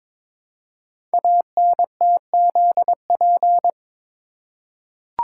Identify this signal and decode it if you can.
Morse code, heard as a keyed tone, about 22 wpm, 700 Hz. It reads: ANTZP
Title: Code Groups - 5 Letters - 22wpm